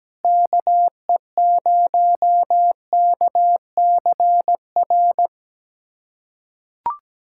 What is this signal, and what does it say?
Morse code, heard as a keyed tone, about 17 wpm, 700 Hz. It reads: KE0KCR
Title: USA Callsign DE Callsign - 6 Characters - 17wpm